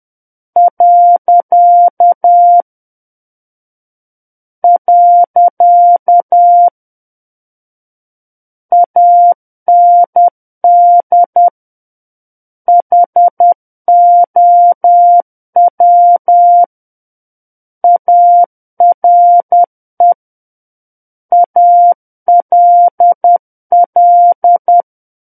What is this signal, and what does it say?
Morse code, heard as a keyed tone, about 10 wpm, 700 Hz. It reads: . AND HOW ARE ALL
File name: Book One / 01-Chapter 1 - 10 WPM